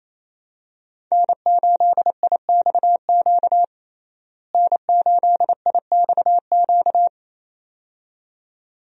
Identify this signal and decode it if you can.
Morse code, heard as a keyed tone, about 28 wpm, 700 Hz. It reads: N8IXQ N8IXQ
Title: USA Call Signs - 5 Characters - Repeated 2x - 28wpm